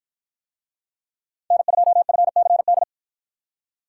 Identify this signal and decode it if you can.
Morse code, heard as a keyed tone, about 48 wpm, 700 Hz. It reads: N2FCD